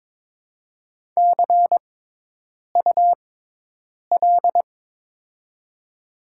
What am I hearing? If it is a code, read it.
Morse code, heard as a keyed tone, about 22 wpm, spaced as for 8 wpm, 700 Hz. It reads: CUL
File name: Top 90 QSO Elements - 6x Character Spacing - 22wpm